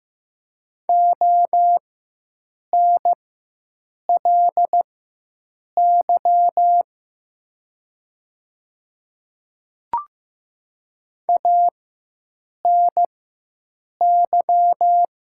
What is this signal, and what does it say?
Morse code, heard as a keyed tone, about 15 wpm, spaced as for 7 wpm, 700 Hz. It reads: ONLY ANY
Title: 4x Character Spacing - Top 200 Words - 15wpm